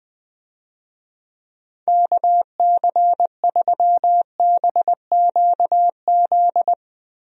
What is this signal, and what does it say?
Morse code, heard as a keyed tone, about 20 wpm, 700 Hz. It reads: KC3BQZ
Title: USA Call Signs - 6 Characters - Repeated 2x - 20wpm